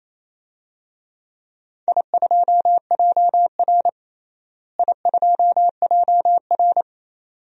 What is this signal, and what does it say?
Morse code, heard as a keyed tone, about 28 wpm, 700 Hz. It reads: I2JR I2JR
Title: DXCC-100 Call Signs - 4 Characters - Repeated 2x - 28wpm